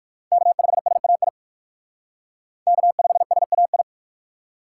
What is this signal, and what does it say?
Morse code, heard as a keyed tone, about 58 wpm, 700 Hz. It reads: K5SAI K5SAI